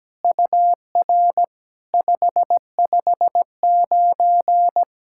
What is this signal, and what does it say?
Morse code, heard as a keyed tone, about 17 wpm, 700 Hz. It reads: UR 559
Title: SOTA QSO Practice 17wpm